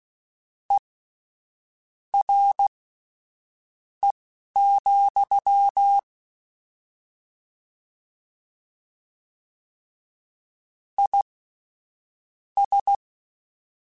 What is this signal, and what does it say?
Morse code, heard as a keyed tone, about 16 wpm, spaced as for 6 wpm, 800 Hz. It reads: ERE, IS
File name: General / General-mp3s / Suffixes sp @10